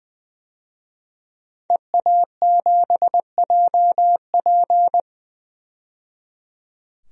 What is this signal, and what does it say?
Morse code, heard as a keyed tone, about 20 wpm, 700 Hz. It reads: EA7JP
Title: DXCC-100 Call Signs - 5 Characters - 20wpm